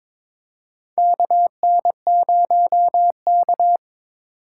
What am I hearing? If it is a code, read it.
Morse code, heard as a keyed tone, about 22 wpm, 700 Hz. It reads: KN0K
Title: USA Call Signs - 4 Characters - 8x Character Spacing - 22wpm